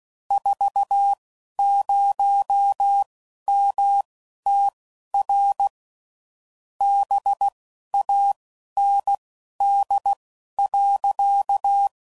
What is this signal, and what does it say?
Morse code, heard as a keyed tone, about 16 wpm, spaced as for 11 wpm, 800 Hz. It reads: 40MTR BAND.
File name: CW Code Groups @10WPM / QSOs / QSO kc7sac